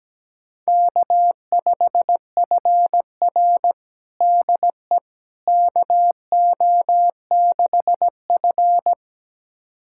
Morse code, heard as a keyed tone, about 17 wpm, 700 Hz. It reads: K5FR DE KO6F
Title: USA Callsign DE Callsign - 4 Characters - 17wpm